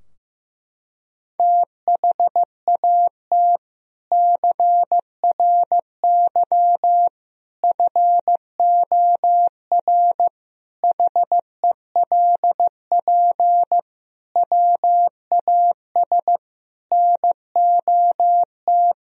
Morse code, heard as a keyed tone, about 15 wpm, 700 Hz. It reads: THAT CRY FOR HELP WAS NOT
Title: Sentences from Top 1000 Words - Review Part 3 - 15wpm